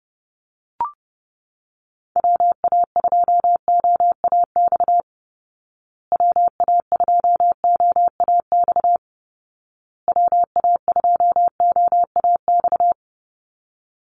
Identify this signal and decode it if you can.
Morse code, heard as a keyed tone, about 30 wpm, 700 Hz. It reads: WA2OAX WA2OAX WA2OAX
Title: Call Signs Repeated 2x - 30wpm